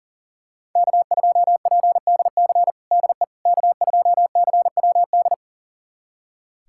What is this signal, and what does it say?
Morse code, heard as a keyed tone, about 40 wpm, 700 Hz. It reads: K2PDC DE K2CWD
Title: USA Callsign DE Callsign - 5 Characters - 40wpm